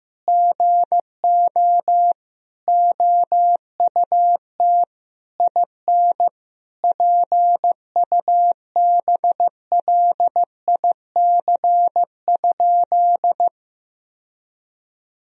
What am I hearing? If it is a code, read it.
Morse code, heard as a keyed tone, about 15 wpm, 700 Hz. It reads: GO OUT IN PUBLIC?
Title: Sentences from Top 1000 Words - Review Part 3 - 15wpm